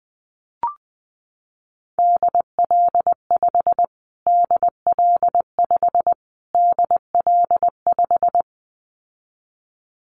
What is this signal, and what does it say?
Morse code, heard as a keyed tone, about 20 wpm, 700 Hz. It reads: DL5 DL5 DL5
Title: Callsign Prefixes for Top 25 DXCC - Repeated 3x - 20wpm